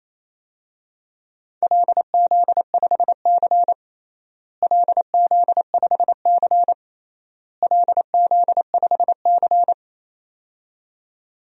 Morse code, heard as a keyed tone, about 27 wpm, 700 Hz. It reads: LZ5C LZ5C LZ5C
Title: DXCC-100 Call Signs - 4 Characters - Repeated 3x - 28wpm